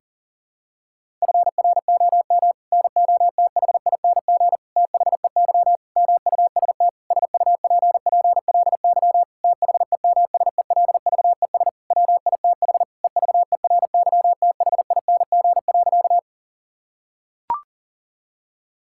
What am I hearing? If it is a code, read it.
Morse code, heard as a keyed tone, about 40 wpm, 700 Hz. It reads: FROM NOTHING THEY MUST SUPPLY THEMSELVES WITH EVERYTHING.
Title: Sentences from Top 1000 Words - Review Part 5 - 40wpm